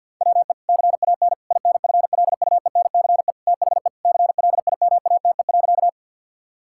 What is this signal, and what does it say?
Morse code, heard as a keyed tone, about 50 wpm, 700 Hz. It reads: WE CAN INFLUENCE THE CLIMATE.
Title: Sentences from Top 1600 Words 50wpm